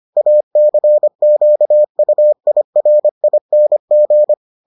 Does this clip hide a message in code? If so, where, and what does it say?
Yes: Morse code from 0.2 to 4.3 s, heard as a keyed tone, about 25 wpm, 600 Hz. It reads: ACQUIRING